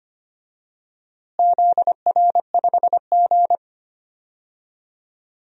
Morse code, heard as a keyed tone, about 25 wpm, 700 Hz. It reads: ZR5G
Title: DXCC-100 Call Signs - 4 Characters - Repeated 2x - 25wpm